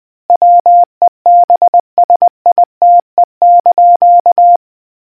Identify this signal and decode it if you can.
Morse code, heard as a keyed tone, about 20 wpm, 700 Hz. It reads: WEBSITE)